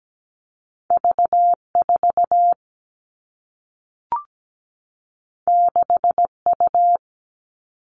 Morse code, heard as a keyed tone, about 17 wpm, 700 Hz. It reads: V4 6U
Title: All Cut Numbers - 2-Digit 17wpm